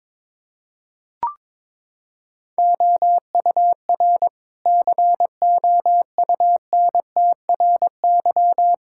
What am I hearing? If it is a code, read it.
Morse code, heard as a keyed tone, about 22 wpm, 700 Hz. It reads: OUR COUNTRY
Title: Sentences from Top 1000 Words - Review Part 5 - 22wpm